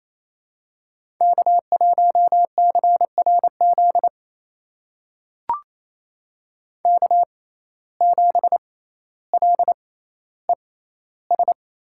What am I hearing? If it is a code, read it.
Morse code, heard as a keyed tone, about 28 wpm, spaced as for 14 wpm, 700 Hz. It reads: K1CRZ K7LES
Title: USA Call Signs - 6x Character Spacing - 28wpm